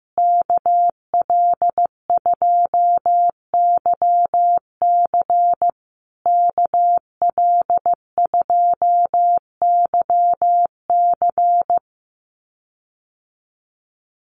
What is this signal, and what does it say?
Morse code, heard as a keyed tone, about 15 wpm, 700 Hz. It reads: KL2YC KL2YC
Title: USA Call Signs Repeated 3x 15wpm